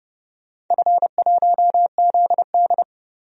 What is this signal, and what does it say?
Morse code, heard as a keyed tone, about 32 wpm, 700 Hz. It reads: F1ZD
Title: DXCC-100 Call Signs - 4 Characters - 30wpm